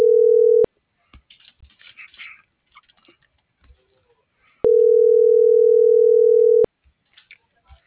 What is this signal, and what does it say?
Ringback tone (North America), 440 + 480 Hz.
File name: CENTRAL SAGOT / out edgar.sagot